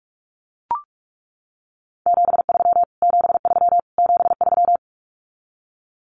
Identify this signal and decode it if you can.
Morse code, heard as a keyed tone, about 44 wpm, 700 Hz. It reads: 73 73 73
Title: Abbreviations Repeated 3x - 45wpm